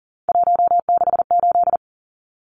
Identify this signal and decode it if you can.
Morse code, heard as a keyed tone, about 40 wpm, 700 Hz. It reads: 168
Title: Numbers - 3-Digit 40wpm